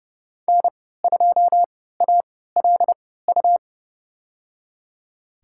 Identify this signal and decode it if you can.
Morse code, heard as a keyed tone, about 32 wpm, spaced as for 17 wpm, 700 Hz. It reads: N2ALU
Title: USA Call Signs - 3x Character Spacing - 30wpm